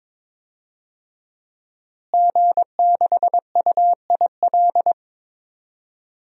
Morse code, heard as a keyed tone, about 22 wpm, 700 Hz. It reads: G6UIL
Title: DXCC-100 Call Signs - 5 Characters - Repeated 3x - 22wpm